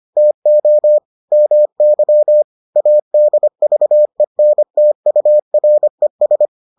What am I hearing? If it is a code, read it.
Morse code, heard as a keyed tone, about 25 wpm, 600 Hz. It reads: TO MY ADVENTURES